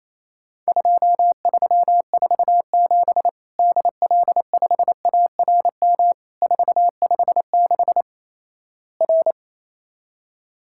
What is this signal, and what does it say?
Morse code, heard as a keyed tone, about 28 wpm, 700 Hz. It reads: 2347 DL5ARM 456 R
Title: Worked All Europe DX Contest - QTC Exchange - 28wpm